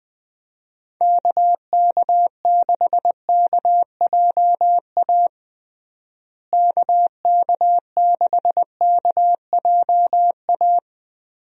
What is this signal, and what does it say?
Morse code, heard as a keyed tone, about 21 wpm, 700 Hz. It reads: KK6KJA KK6KJA
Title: USA Call Signs - 6 Characters - Repeated 3x - 20wpm